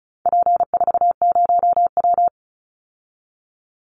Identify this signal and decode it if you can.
Morse code, heard as a keyed tone, about 34 wpm, 700 Hz. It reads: P40W
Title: Call Signs Repeated 2x - 35wpm